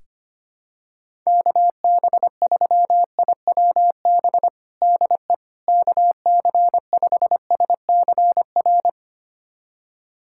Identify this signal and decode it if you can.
Morse code, heard as a keyed tone, about 25 wpm, 700 Hz. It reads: KB3IWB DE KC5SCR
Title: USA Callsign DE Callsign - 6 Characters - 25wpm